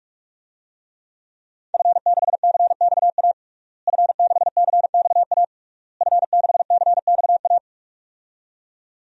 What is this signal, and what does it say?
Morse code, heard as a keyed tone, about 46 wpm, 700 Hz. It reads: F6CXA F6CXA F6CXA
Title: DXCC-100 Call Signs - 5 Characters - Repeated 3x - 45wpm